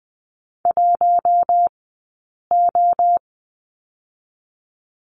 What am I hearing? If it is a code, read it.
Morse code, heard as a keyed tone, about 20 wpm, 700 Hz. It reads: 1 O
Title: 073 Lesson 13d - Letter L - Two Characters 20wpm